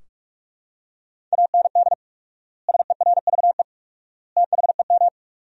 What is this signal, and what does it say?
Morse code, heard as a keyed tone, about 46 wpm, 700 Hz. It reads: AND SERVE THEM
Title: 4x Word-Spacing - Sentences from Top 400 Words - 45wpm